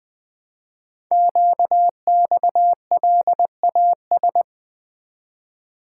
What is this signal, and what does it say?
Morse code, heard as a keyed tone, about 20 wpm, 700 Hz. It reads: QXLAS
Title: Code Groups - 5 Letters - 20wpm